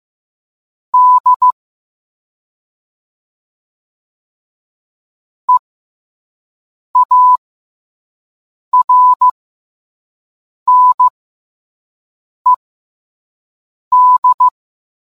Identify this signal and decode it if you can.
Morse code, heard as a keyed tone, about 15 wpm, spaced as for 5 wpm, 1000 Hz. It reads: D EARNED